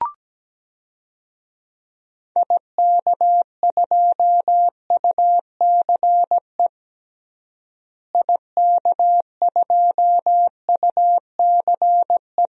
Morse code, heard as a keyed tone, about 17 wpm, 700 Hz. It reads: IK2UCE IK2UCE
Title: DXCC-100 Call Signs - 6 Characters - Repeated 3x - 17wpm